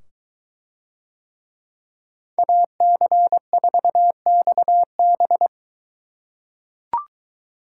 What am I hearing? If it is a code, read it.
Morse code, heard as a keyed tone, about 22 wpm, 700 Hz. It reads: AC4XB